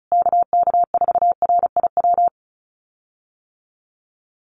Morse code, heard as a keyed tone, about 34 wpm, 700 Hz. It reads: KK4RIW
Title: USA Call Signs Repeated 3x 35wpm